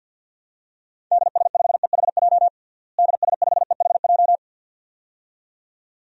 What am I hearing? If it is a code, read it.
Morse code, heard as a keyed tone, about 52 wpm, 700 Hz. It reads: DS5EHJ DS5EHJ
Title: DXCC-100 Call Signs - 6 Characters - Repeated 2x - 50wpm